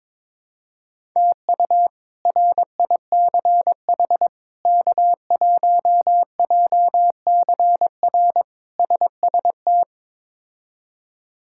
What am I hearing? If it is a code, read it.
Morse code, heard as a keyed tone, about 22 wpm, 700 Hz. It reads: TU RICH K1JCR SST
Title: K1USN SST QSO Practice - 22wpm